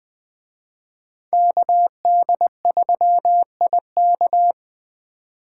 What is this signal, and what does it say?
Morse code, heard as a keyed tone, about 20 wpm, 700 Hz. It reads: KD3IK
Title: Speed Racing - USA 2x2 Call Signs 17wpm